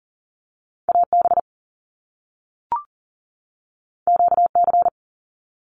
Morse code, heard as a keyed tone, about 40 wpm, 700 Hz. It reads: AB QC
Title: Canadian Provinces and Territories 40wpm